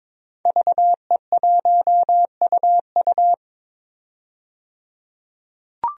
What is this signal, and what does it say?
Morse code, heard as a keyed tone, about 22 wpm, 700 Hz. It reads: VE1UU
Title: DXCC-100 Call Signs - 5 Characters - 22wpm